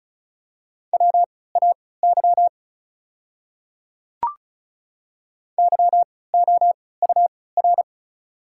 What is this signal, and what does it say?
Morse code, heard as a keyed tone, about 36 wpm, spaced as for 20 wpm, 700 Hz. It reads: WAY YOUR
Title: 3x Character Spacing - Top 100 Words - 35wpm